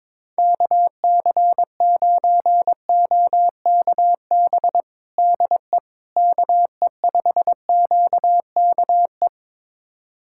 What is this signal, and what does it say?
Morse code, heard as a keyed tone, about 22 wpm, 700 Hz. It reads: KC9OKB DE KE5QKE
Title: USA Callsign DE Callsign - 6 Characters - 22wpm